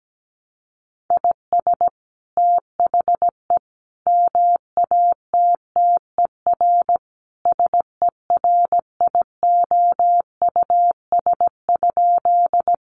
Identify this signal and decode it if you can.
Morse code, heard as a keyed tone, about 17 wpm, 700 Hz. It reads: IS THE MATTER SERIOUS?
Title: Sentences from Top 1100 Words 17wpm